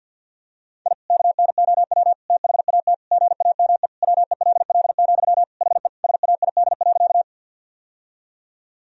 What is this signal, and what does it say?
Morse code, heard as a keyed tone, about 50 wpm, 700 Hz. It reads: I KNOW THAT GAME WELL, HE SAID.
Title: Sentences from Top 1000 Words - Review Part 3 - 50wpm